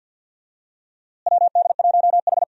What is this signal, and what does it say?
Morse code, heard as a keyed tone, about 48 wpm, 700 Hz. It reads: WD1H
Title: USA Call Signs - 4 Characters - Repeated 2x - 50wpm